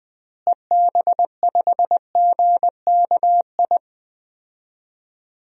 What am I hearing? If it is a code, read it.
Morse code, heard as a keyed tone, about 20 wpm, 700 Hz. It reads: EB5GKI
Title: DXCC-100 Call Signs - 6 Characters - Repeated 2x - 20wpm